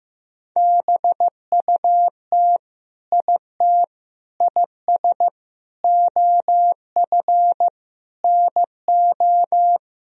Morse code, heard as a keyed tone, about 16 wpm, 700 Hz. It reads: BUT IT IS OF NO
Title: Sentences from Top 1000 Words - Review Part 2 - 15wpm